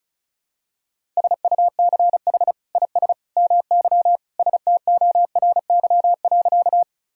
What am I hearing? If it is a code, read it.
Morse code, heard as a keyed tone, about 34 wpm, 700 Hz. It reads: SUCH IS MY STORY.
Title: Sentences from Top 1000 Words - Review Part 1 - 35wpm